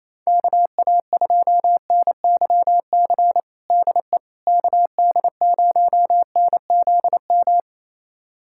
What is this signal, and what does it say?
Morse code, heard as a keyed tone, about 28 wpm, 700 Hz. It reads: KA2NYC DE KD0NZM
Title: USA Callsign DE Callsign - 6 Characters - 28wpm